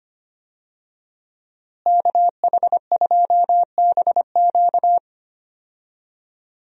Morse code, heard as a keyed tone, about 25 wpm, 700 Hz. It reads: KH2BQ
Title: USA Call Signs - 5 Characters - Repeated 2x - 25wpm